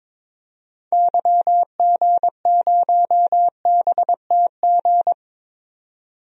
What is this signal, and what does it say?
Morse code, heard as a keyed tone, about 22 wpm, 700 Hz. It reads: YG0BTG
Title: DXCC-100 Call Signs - 6 Characters - 22wpm